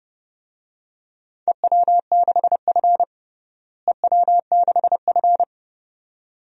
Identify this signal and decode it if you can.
Morse code, heard as a keyed tone, about 30 wpm, 700 Hz. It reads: EW6F EW6F
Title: DXCC-100 Call Signs - 4 Characters - Repeated 2x - 30wpm